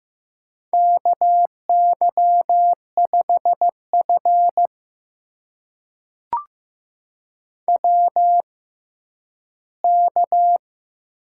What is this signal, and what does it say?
Morse code, heard as a keyed tone, about 15 wpm, spaced as for 10 wpm, 700 Hz. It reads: KY5F WK
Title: USA Call Signs - 4 Characters - 6x Character Spacing - 15wpm